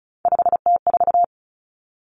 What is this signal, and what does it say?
Morse code, heard as a keyed tone, about 34 wpm, 700 Hz. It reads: 5T4
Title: Common Cut Numbers - 3-Digit 35wpm